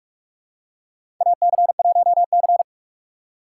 Morse code, heard as a keyed tone, about 46 wpm, 700 Hz. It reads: AC1C
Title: USA Call Signs - 3x Character Spacing - 45wpm